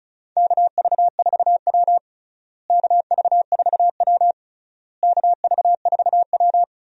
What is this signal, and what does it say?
Morse code, heard as a keyed tone, about 36 wpm, 700 Hz. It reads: KV4W KV4W KV4W
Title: USA Call Signs - 4 Characters - Repeated 3x - 35wpm